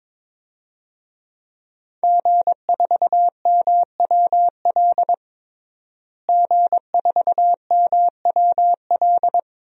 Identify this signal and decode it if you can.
Morse code, heard as a keyed tone, about 22 wpm, 700 Hz. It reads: G4MWL G4MWL
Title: DXCC-100 Call Signs - 5 Characters - Repeated 2x - 22wpm